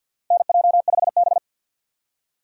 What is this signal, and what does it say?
Morse code, heard as a keyed tone, about 50 wpm, 700 Hz. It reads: NJ5B
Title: USA Call Signs - 4 Characters - 50wpm